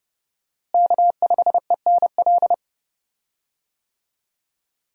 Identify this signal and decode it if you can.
Morse code, heard as a keyed tone, about 30 wpm, 700 Hz. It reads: K5ENL
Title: USA Call Signs - 5 Characters - Repeated 3x - 30wpm